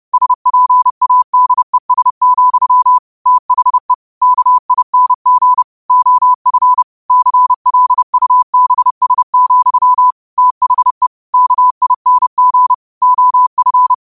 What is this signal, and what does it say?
Morse code, heard as a keyed tone, about 30 wpm, 1000 Hz. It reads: SPADES, THE KING OF CLUBS, THE KING OF
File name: qotdfiles / 30wpm